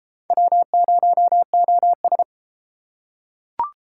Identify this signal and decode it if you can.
Morse code, heard as a keyed tone, about 32 wpm, 700 Hz. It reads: W0OS